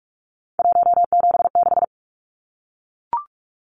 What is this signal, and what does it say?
Morse code, heard as a keyed tone, about 44 wpm, 700 Hz. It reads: 176
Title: Numbers - 3-Digit 45wpm